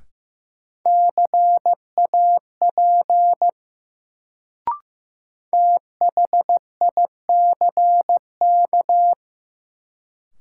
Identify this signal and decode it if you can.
Morse code, heard as a keyed tone, about 15 wpm, 700 Hz. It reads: CAP THICK
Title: Sentences from Top 1600 Words - New Word Drill Alt - 15wpm